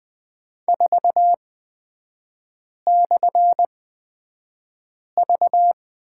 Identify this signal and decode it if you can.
Morse code, heard as a keyed tone, about 20 wpm, spaced as for 5 wpm, 700 Hz. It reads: 4/V
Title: 224 Lesson 36e - BT - Three Characters 20wpm